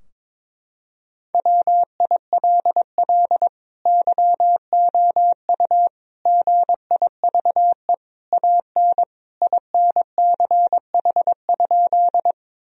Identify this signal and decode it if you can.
Morse code, heard as a keyed tone, about 22 wpm, 700 Hz. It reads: WILL YOU GIVE AN INCH?
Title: Sentences from Top 1000 Words - Review Part 3 - 22wpm